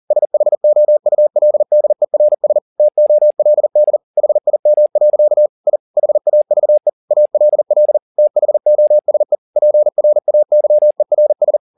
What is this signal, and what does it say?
Morse code, heard as a keyed tone, about 40 wpm, 600 Hz. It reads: SHOULDERS TOLD HIM. I HAVE ALL THOSE PRAYERS
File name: the old man and the sea 2 40wpm